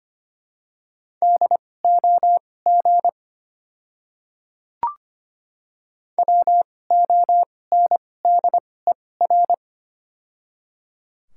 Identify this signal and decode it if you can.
Morse code, heard as a keyed tone, about 25 wpm, spaced as for 18 wpm, 700 Hz. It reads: DOG WONDER
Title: 2x Character Spacing - Top 500 Words - 25wpm